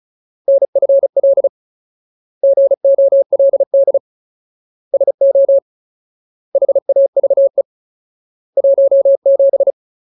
Morse code, heard as a keyed tone, about 34 wpm, 550 Hz. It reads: NFL GOLD SO HAVE 17
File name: Ultimate Word List / 35WPM Ultimate